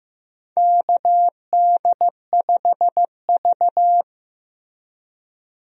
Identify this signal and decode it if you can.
Morse code, heard as a keyed tone, about 15 wpm, 700 Hz. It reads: KD5V
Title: USA Call Signs - 4 Characters - Repeated 3x - 15wpm